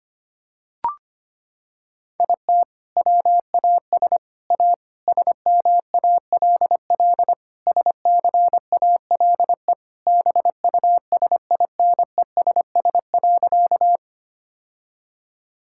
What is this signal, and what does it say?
Morse code, heard as a keyed tone, about 25 wpm, 700 Hz. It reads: IT WAS A SMALL SCALE BUSINESS.
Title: Sentences from Top 1000 Words - Review Part 5 - 25wpm